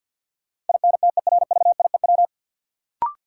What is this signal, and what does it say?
Morse code, heard as a keyed tone, about 50 wpm, 700 Hz. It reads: INTERVIEW
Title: Sentences from Top 1600 Words - New Word Drill - 50wpm